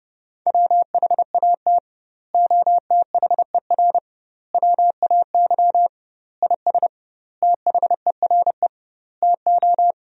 Morse code, heard as a keyed tone, about 30 wpm, 700 Hz. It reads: WHAT OTHER WAY IS THERE TO
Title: 2x Word-Spacing - Sentences from Top 100 Words - 30wpm